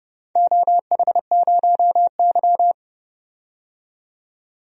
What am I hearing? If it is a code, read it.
Morse code, heard as a keyed tone, about 30 wpm, 700 Hz. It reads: OH0Y
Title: DXCC-100 Call Signs - 4 Characters - 30wpm